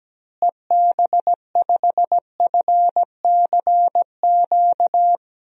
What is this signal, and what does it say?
Morse code, heard as a keyed tone, about 17 wpm, 700 Hz. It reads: EB5FCQ
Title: DXCC-100 Call Signs - 6 Characters - Repeated 2x - 17wpm